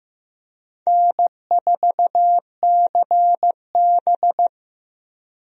Morse code, heard as a keyed tone, about 15 wpm, 700 Hz. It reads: N4CB
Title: USA Call Signs - 4 Characters - Repeated 3x - 15wpm